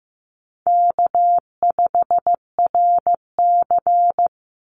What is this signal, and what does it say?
Morse code, heard as a keyed tone, about 15 wpm, 700 Hz. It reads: K5RC
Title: Easy Call Signs Repeated 2x - 15wpm